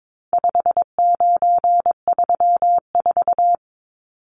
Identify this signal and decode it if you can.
Morse code, heard as a keyed tone, about 22 wpm, 700 Hz. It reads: 5934
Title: Numbers - 4-Digit 22wpm